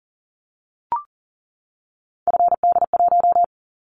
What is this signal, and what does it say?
Morse code, heard as a keyed tone, about 40 wpm, 700 Hz. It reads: FD1
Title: Callsign Prefixes for Top 50 DXCC - 40wpm